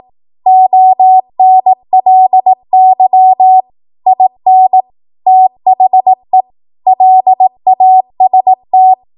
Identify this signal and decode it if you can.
Morse code, heard as a keyed tone, about 18 wpm, 750 Hz. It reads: ONLY IN THE LAST
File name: Morse / 18WPM